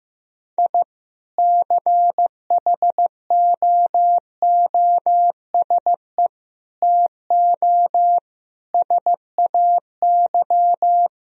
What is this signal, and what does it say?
Morse code, heard as a keyed tone, about 15 wpm, 700 Hz. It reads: I CHOOSE TO SAY
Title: Sentences from Top 1000 Words - Review Part 4 - 15wpm